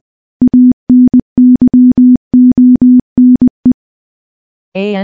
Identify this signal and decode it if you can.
Morse code, heard as a keyed tone, about 20 wpm, 250 Hz. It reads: ANYONE
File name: morse-training-20WPM